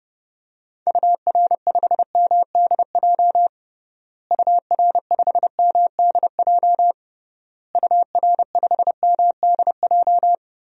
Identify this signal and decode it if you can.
Morse code, heard as a keyed tone, about 32 wpm, 700 Hz. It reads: UR5MDJ UR5MDJ UR5MDJ
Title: DXCC-100 Call Signs - 6 Characters - Repeated 3x - 30wpm